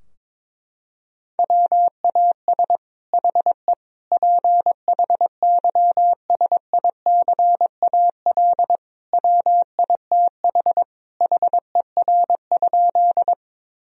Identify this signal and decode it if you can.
Morse code, heard as a keyed tone, about 22 wpm, 700 Hz. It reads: WAS HE PHYSICAL WITH HER?
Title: Sentences from Top 1000 Words - Review Part 4 - 22wpm